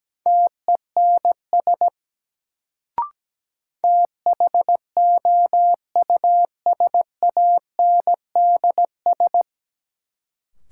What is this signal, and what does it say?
Morse code, heard as a keyed tone, about 17 wpm, 700 Hz. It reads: TENS THOUSANDS
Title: Sets of 4 Words - New Word Drill - 17wpm